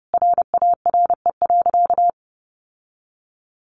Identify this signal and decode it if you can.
Morse code, heard as a keyed tone, about 30 wpm, 700 Hz. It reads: RARE.
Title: Sentences from Top 1100 Words 30wpm